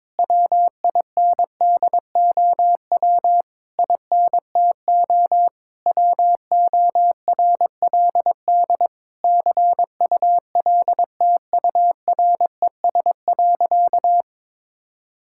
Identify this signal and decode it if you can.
Morse code, heard as a keyed tone, about 21 wpm, 700 Hz. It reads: WINDOW INTO WORLD CULTURES.
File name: Difficult Sentences with Hints - 22wpm